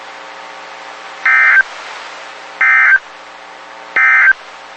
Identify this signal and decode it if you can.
Emergency Alert System (SAME) tones, heard as harsh buzzing bursts. It is heard from 1.1 to 4.4 s.
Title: WXK91 NOAA Weather Radio